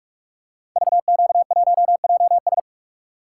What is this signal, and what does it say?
Morse code, heard as a keyed tone, about 44 wpm, 700 Hz. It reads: VQ1JS
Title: Code Groups - 5 Letter:Numbers - 45wpm